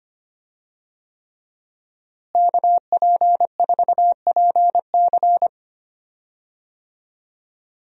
Morse code, heard as a keyed tone, about 25 wpm, 700 Hz. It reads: KP4PC
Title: DXCC-100 Call Signs - 5 Characters - Repeated 3x - 25wpm